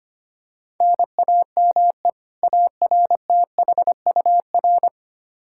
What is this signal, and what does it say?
Morse code, heard as a keyed tone, about 25 wpm, 700 Hz. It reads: NAME ARTHUR
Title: Practice QSO - Exchange 1 - Beginning - 25wpm